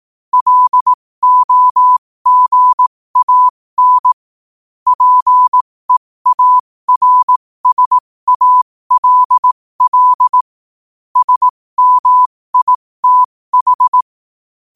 Morse code, heard as a keyed tone, about 18 wpm, spaced as for 15 wpm, 1000 Hz. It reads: LOGAN PEARSALL SMITH
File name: qotdfiles / 15wpm